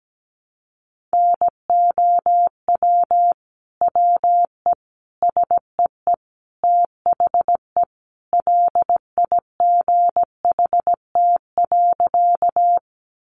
Morse code, heard as a keyed tone, about 17 wpm, 700 Hz. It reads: NOW WE SEE THE LIGHT.
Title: Sentences from Top 200 Words 17wpm